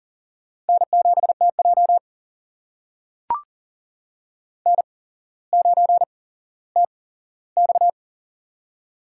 Morse code, heard as a keyed tone, about 40 wpm, spaced as for 17 wpm, 700 Hz. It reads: N7TJ N9TX
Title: USA Call Signs - 4 Characters - 8x Character Spacing - 40wpm